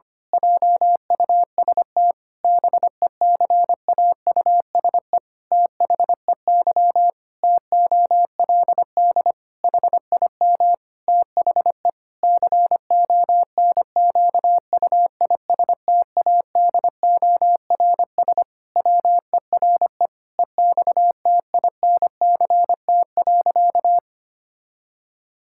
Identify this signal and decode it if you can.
Morse code, heard as a keyed tone, about 25 wpm, 700 Hz. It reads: JUST BECAUSE THEY TOLD HIM THE CONQUISTADORS WERE EXTINCT.
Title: Pangram Challenge v2 - 25wpm